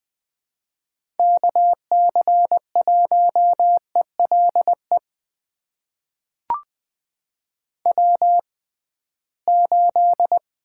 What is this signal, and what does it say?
Morse code, heard as a keyed tone, about 20 wpm, spaced as for 15 wpm, 700 Hz. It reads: KC1ELE W8
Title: USA Call Signs - 6x Character Spacing - 20wpm